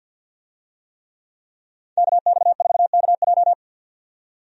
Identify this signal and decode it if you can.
Morse code, heard as a keyed tone, about 50 wpm, 700 Hz. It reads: KX4KJ
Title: USA Call Signs - 5 Characters - 50wpm